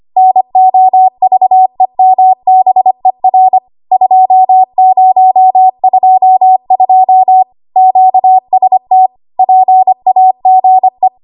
Morse code, heard as a keyed tone, about 25 wpm, 750 Hz. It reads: NOVEMBER 2022 QST PAGE
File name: Archive / 25 WPM / 25WPM